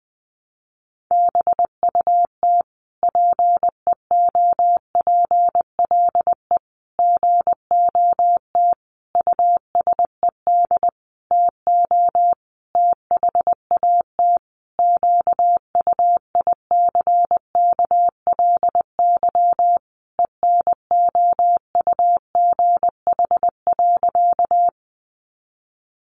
Morse code, heard as a keyed tone, about 19 wpm, 700 Hz. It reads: BUT PEOPLE GOT USED TO THAT QUICKLY ENOUGH.
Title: Sentences from Top 700 Words 20wpm